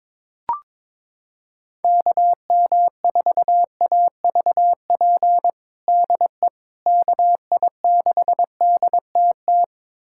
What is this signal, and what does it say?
Morse code, heard as a keyed tone, about 22 wpm, 700 Hz. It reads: KM4AVP DE KI6DTT
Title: USA Callsign DE Callsign - 6 Characters - 22wpm